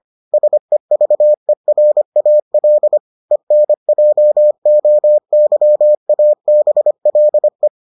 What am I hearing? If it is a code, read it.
Morse code, heard as a keyed tone, about 25 wpm, 600 Hz. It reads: SEVERAL ENJOYABLE